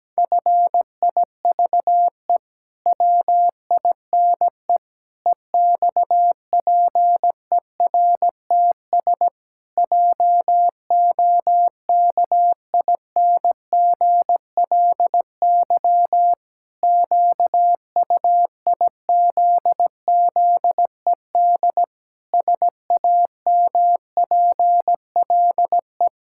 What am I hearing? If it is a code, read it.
Morse code, heard as a keyed tone, about 17 wpm, 700 Hz. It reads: FIVE WINE EXPERTS JOKINGLY QUIZZED SAMPLE
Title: Pangram Challenge v2 - 17wpm